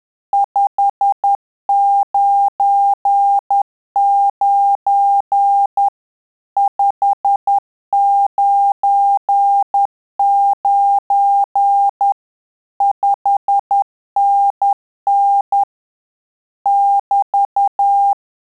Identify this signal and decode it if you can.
Morse code, heard as a keyed tone, about 10 wpm, 800 Hz. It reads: 599 599 5NN <BT>
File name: Telegraphy-group / QSO CW 10 PPM